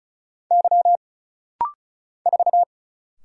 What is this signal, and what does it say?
Morse code, heard as a keyed tone, about 34 wpm, 700 Hz. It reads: Y 4
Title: Single Letter-Number Alt - Rapid-Fire 35wpm